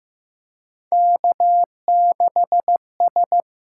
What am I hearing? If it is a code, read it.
Morse code, heard as a keyed tone, about 15 wpm, 700 Hz. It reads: K6S
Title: USA Call Signs - 4 Characters - 8x Character Spacing - 15wpm